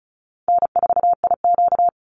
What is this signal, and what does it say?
Morse code, heard as a keyed tone, about 34 wpm, 700 Hz. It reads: N4IQ
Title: Easy Call Signs Repeated 3x - 35wpm